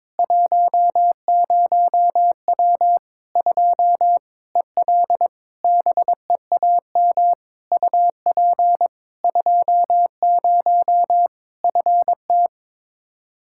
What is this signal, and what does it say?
Morse code, heard as a keyed tone, about 22 wpm, 700 Hz. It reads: 10W 2 EL BEAM UP 20 FT
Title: Practice QSO - Exchange 2 - Info - 22wpm